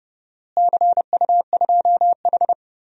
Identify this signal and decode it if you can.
Morse code, heard as a keyed tone, about 30 wpm, 700 Hz. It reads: CU2H
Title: DXCC-100 Call Signs - 4 Characters - Repeated 3x - 30wpm